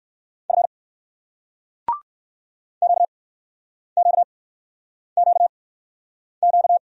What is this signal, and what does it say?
Morse code, heard as a keyed tone, about 60 wpm, spaced as for 9 wpm, 700 Hz. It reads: F QQQQ